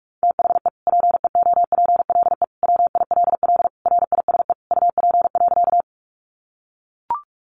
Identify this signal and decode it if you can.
Morse code, heard as a keyed tone, about 44 wpm, 700 Hz. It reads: THE PEOPLE WILL RISE UP.
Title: Sentences from Top 900 Words 45wpm